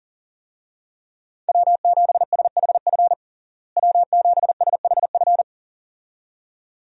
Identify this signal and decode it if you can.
Morse code, heard as a keyed tone, about 40 wpm, 700 Hz. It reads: W7SHF W7SHF
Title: USA Call Signs - 5 Characters - Repeated 2x - 40wpm